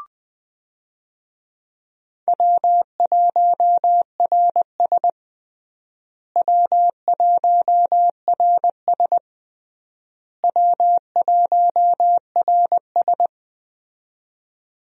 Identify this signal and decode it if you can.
Morse code, heard as a keyed tone, about 20 wpm, 700 Hz. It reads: W1RS W1RS W1RS
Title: USA Call Signs - 4 Characters - Repeated 3x - 20wpm